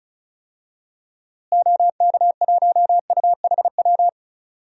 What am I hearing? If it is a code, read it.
Morse code, heard as a keyed tone, about 34 wpm, 700 Hz. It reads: OK1UHW
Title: DXCC-100 Call Signs - 6 Characters - Repeated 3x - 35wpm